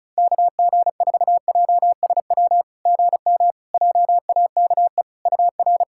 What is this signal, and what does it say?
Morse code, heard as a keyed tone, about 34 wpm, 690 Hz. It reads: KG4JSW GM JAKE UR
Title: SOTA QSO Practice 35wpm